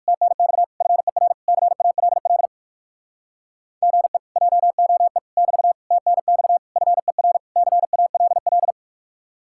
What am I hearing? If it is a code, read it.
Morse code, heard as a keyed tone, about 58 wpm, 700 Hz. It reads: TNX FER CALL GE JOE <BT> TNX FER CALL